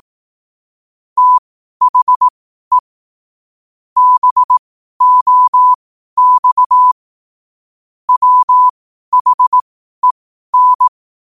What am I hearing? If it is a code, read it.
Morse code, heard as a keyed tone, about 18 wpm, spaced as for 12 wpm, 1000 Hz. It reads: THE BOX WHEN
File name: qotdfiles / 12wpm